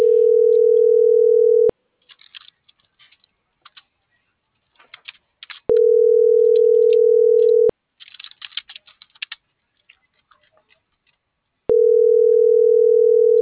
Ringback tone (North America), 440 + 480 Hz.
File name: CENTRAL MULTIMODAL / out alfredovelazquez